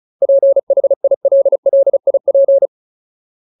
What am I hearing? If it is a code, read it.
Morse code, heard as a keyed tone, about 34 wpm, 550 Hz. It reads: PHILLIP
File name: Ultimate Word List / 35WPM Ultimate